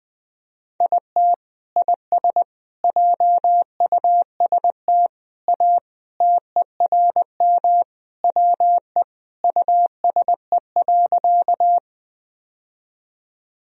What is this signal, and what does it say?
Morse code, heard as a keyed tone, about 20 wpm, 700 Hz. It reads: IT IS JUST A TERM WE USE.
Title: Sentences from Top 1000 Words - Review Part 4 - 20wpm